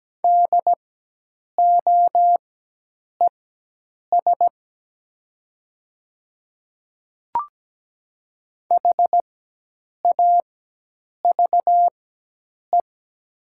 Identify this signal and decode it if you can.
Morse code, heard as a keyed tone, about 17 wpm, spaced as for 8 wpm, 700 Hz. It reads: DOES HAVE
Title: 4x Character Spacing - Top 300 Words - 17wpm